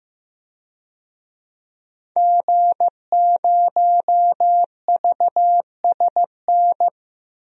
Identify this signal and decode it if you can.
Morse code, heard as a keyed tone, about 15 wpm, 700 Hz. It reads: G0VSN
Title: DXCC-100 Call Signs - 5 Characters - 15wpm